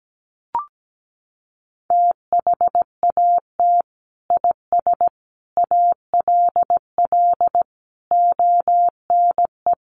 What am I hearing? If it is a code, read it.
Morse code, heard as a keyed tone, about 17 wpm, 700 Hz. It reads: THAT IS ALL ONE
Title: Sentences from Top 200 Words 17wpm